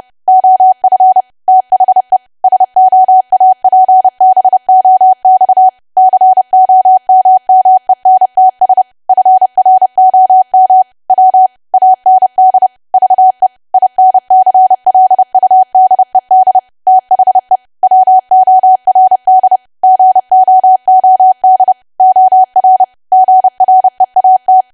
Morse code, heard as a keyed tone, about 30 wpm, 750 Hz. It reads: OF THE SOAPBOX COMMENTS FROM W AND VE INCLUDED THE WORD GOOD OR GREAT
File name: cwa / W1AW-Level3 / 30WPM